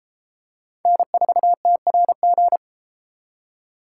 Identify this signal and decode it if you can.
Morse code, heard as a keyed tone, about 32 wpm, 700 Hz. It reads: N4TRG